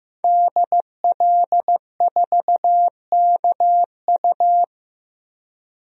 Morse code, heard as a keyed tone, about 15 wpm, 700 Hz. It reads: DL4KU
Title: DXCC-100 Call Signs - 5 Characters - 15wpm